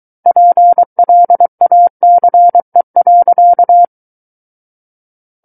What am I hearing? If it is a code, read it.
Morse code, heard as a keyed tone, about 23 wpm, 700 Hz. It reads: PLACE.
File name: Book One / 14-Chapter 14 - 23 WPM